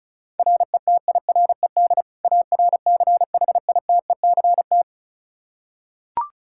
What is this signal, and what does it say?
Morse code, heard as a keyed tone, about 36 wpm, 700 Hz. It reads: RETIRED ARCHITECT
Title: Practice QSO - Exchange 3 - More Info - 35wpm